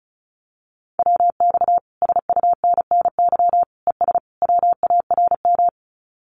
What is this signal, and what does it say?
Morse code, heard as a keyed tone, about 34 wpm, 700 Hz. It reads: WX SUNNY ES WARM
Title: Common QSO Phrases Encore 35wpm